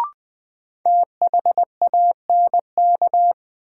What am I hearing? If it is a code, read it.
Morse code, heard as a keyed tone, about 20 wpm, 700 Hz. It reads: THANK
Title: Sets of 2 Words - New Word Drill - 20wpm